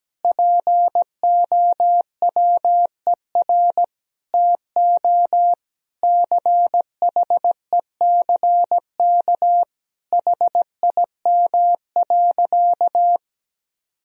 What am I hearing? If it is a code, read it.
Morse code, heard as a keyed tone, about 17 wpm, 700 Hz. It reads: POWER TO CHECK HIM.
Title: Sentences from Top 1000 Words - Review Part 3 - 17wpm